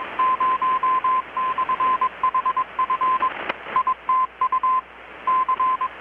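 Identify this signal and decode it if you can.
Morse code, heard as a keyed tone, about 21 wpm, 1030 Hz, with radio noise. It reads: 0/HF ITU C
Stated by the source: Morse from KPH's Night of Nights XX maritime radiotelegraphy event, received in Chiapas, Mexico about 3,800 km away on a Kenwood R-600 with a grounded 10 m long-wire antenna.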